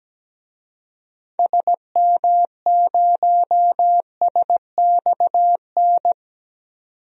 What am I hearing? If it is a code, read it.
Morse code, heard as a keyed tone, about 17 wpm, 700 Hz. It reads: SM0SXN
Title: DXCC-100 Call Signs - 6 Characters - 17wpm